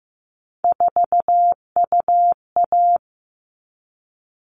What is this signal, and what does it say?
Morse code, heard as a keyed tone, about 15 wpm, 700 Hz. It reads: 4UA
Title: All Cut Numbers - 3-Digit 15wpm